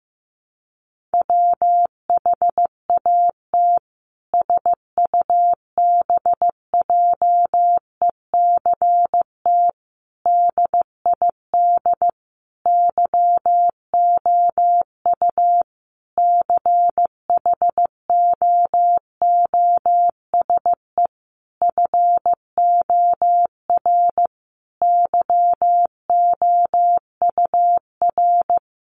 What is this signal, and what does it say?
Morse code, heard as a keyed tone, about 15 wpm, 700 Hz. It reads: WHAT SUBJECT DID YOU CHOOSE FOR YOUR
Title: Sentences from Top 1500 Words 15wpm